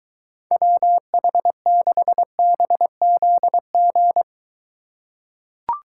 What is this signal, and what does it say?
Morse code, heard as a keyed tone, about 22 wpm, 700 Hz. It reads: WH6BZG